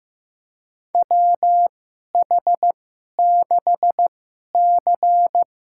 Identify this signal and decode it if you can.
Morse code, heard as a keyed tone, about 15 wpm, spaced as for 11 wpm, 700 Hz. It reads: WH6C
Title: USA Call Signs - 2x Character Spacing - 15wpm